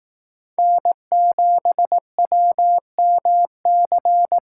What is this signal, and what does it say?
Morse code, heard as a keyed tone, about 18 wpm, 700 Hz. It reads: N7WMC
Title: Speed Racing - USA 1x3 Call Signs 15wpm